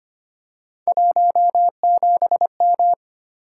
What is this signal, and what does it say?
Morse code, heard as a keyed tone, about 25 wpm, 700 Hz. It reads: 17M
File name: Three Letter-Number Combinations - 25wpm